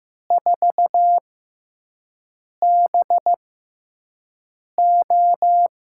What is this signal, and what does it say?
Morse code, heard as a keyed tone, about 15 wpm, spaced as for 5 wpm, 700 Hz. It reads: 4BO
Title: USA Call Signs - 5 Characters - 6x Character Spacing - 15wpm